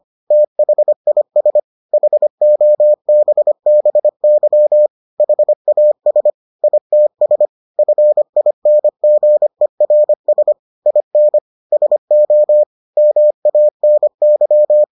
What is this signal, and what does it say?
Morse code, heard as a keyed tone, about 25 wpm, 600 Hz. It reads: THIS HOBBY HAS ITS FINGERS IN SO MANY